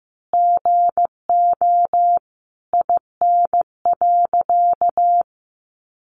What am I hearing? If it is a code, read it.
Morse code, heard as a keyed tone, about 15 wpm, 700 Hz. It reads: GO IN.
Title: Sentences from Top 700 Words 15wpm